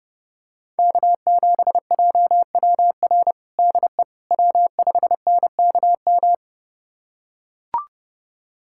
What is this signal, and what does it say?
Morse code, heard as a keyed tone, about 30 wpm, 700 Hz. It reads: K7JWR DE W5NKM
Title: USA Callsign DE Callsign 30wpm